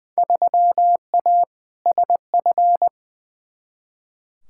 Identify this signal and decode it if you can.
Morse code, heard as a keyed tone, about 20 wpm, 700 Hz. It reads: 3A SF
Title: Field Day Exchanges Encore - 20wpm